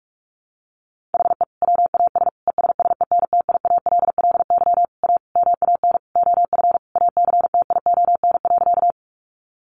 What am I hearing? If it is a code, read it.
Morse code, heard as a keyed tone, about 44 wpm, 700 Hz. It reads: HE WAS ESSENTIALLY A MAN OF ACTION.
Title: Sentences from Top 1300 Words 45wpm